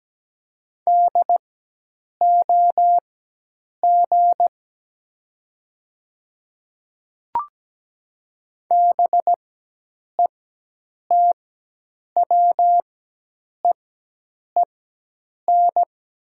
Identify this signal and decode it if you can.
Morse code, heard as a keyed tone, about 17 wpm, spaced as for 8 wpm, 700 Hz. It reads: DOG BETWEEN
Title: 4x Character Spacing - Top 500 Words - 17wpm